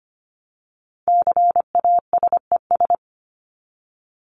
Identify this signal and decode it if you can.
Morse code, heard as a keyed tone, about 25 wpm, 700 Hz. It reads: CASES
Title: ES:S Suffix - Top 100 Words with Base Words 25wpm